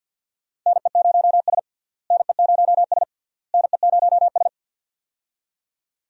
Morse code, heard as a keyed tone, about 52 wpm, 700 Hz. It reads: NE0S NE0S NE0S
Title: USA Call Signs - 4 Characters - Repeated 3x - 50wpm